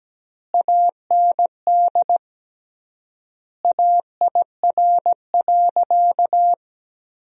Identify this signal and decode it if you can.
Morse code, heard as a keyed tone, about 17 wpm, 700 Hz. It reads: AND AIR.
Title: 3x Word-Spacing - Sentences from Top 400 Words - 17wpm